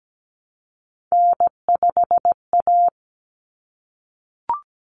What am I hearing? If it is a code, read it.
Morse code, heard as a keyed tone, about 17 wpm, 690 Hz. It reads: N5A
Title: Common Cut Numbers - 3-Digit 17wpm